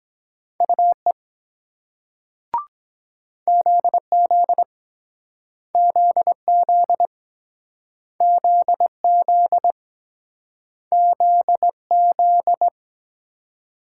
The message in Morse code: UE ZZ ZZ ZZ ZZ

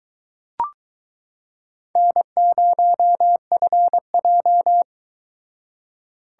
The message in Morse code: N0FJ

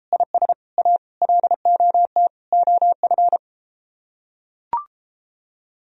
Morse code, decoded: IS A LOT OF

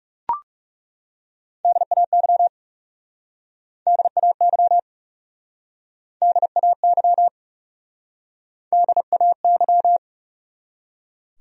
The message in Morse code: DAY DAY DAY DAY

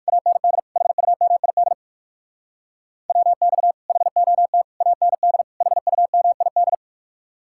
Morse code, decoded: AND HUMID WX HOT AND HUMID

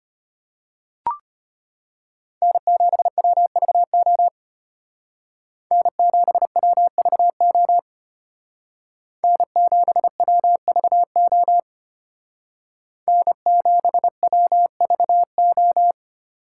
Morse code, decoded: N7WVO N7WVO N7WVO N7WVO

